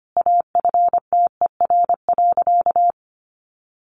Morse code, heard as a keyed tone, about 25 wpm, 700 Hz. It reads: AFTER.
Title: Sentences from Top 500 Words 25wpm